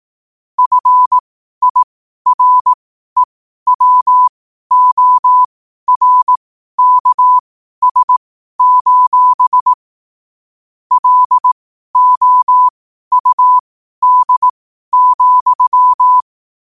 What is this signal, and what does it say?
Morse code, heard as a keyed tone, about 18 wpm, spaced as for 12 wpm, 1000 Hz. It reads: FIREWORKS: LOUD,
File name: qotdfiles / 12wpm